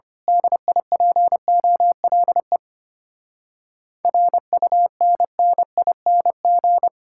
Morse code, heard as a keyed tone, about 24 wpm, 700 Hz. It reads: DIPOLE RUNNING
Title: Speed-Racing Common QSO Phrases Encore - 25wpm